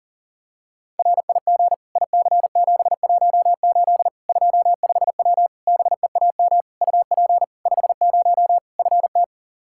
Morse code, heard as a keyed tone, about 40 wpm, 700 Hz. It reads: RIG IC718 25W BEAM UP 50 FT
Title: K1USN SST QSO Practice 40wpm